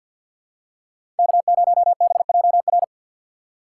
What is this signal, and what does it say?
Morse code, heard as a keyed tone, about 50 wpm, 700 Hz. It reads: K0BJR
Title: USA Call Signs - 5 Characters - 8x Character Spacing - 50wpm